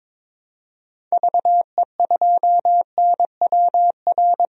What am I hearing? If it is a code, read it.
Morse code, heard as a keyed tone, about 22 wpm, 700 Hz. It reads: VE2NWR
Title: DXCC-100 Call Signs - 6 Characters - 22wpm